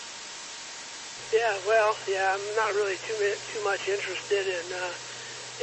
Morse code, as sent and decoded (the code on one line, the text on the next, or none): none